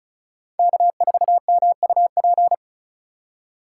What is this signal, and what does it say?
Morse code, heard as a keyed tone, about 34 wpm, 700 Hz. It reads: K4MUP
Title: USA Call Signs - 5 Characters - 2x Character Spacing - 35wpm